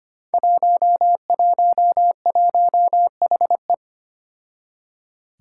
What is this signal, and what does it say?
Morse code, heard as a keyed tone, about 25 wpm, 700 Hz. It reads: 111HE
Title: Five Challenging Letter-Number Combinations 25wpm